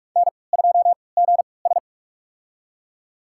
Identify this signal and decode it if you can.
Morse code, heard as a keyed tone, about 46 wpm, spaced as for 26 wpm, 700 Hz. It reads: N2GS
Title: USA Call Signs - 3x Character Spacing - 45wpm